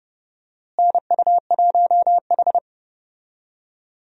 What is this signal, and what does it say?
Morse code, heard as a keyed tone, about 30 wpm, 700 Hz. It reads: NU1H
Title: USA Call Signs - 4 Characters - Repeated 3x - 30wpm